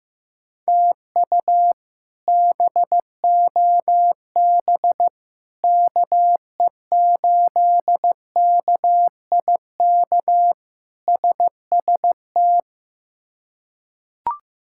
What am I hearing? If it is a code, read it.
Morse code, heard as a keyed tone, about 15 wpm, 700 Hz. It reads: TU BOB KE8KIK SST
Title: K1USN SST QSO Practice - 15wpm